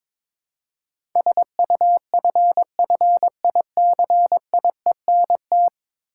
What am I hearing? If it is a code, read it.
Morse code, heard as a keyed tone, about 22 wpm, 700 Hz. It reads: SUFFICIENT
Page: morse-code-ninja-practice-22wpm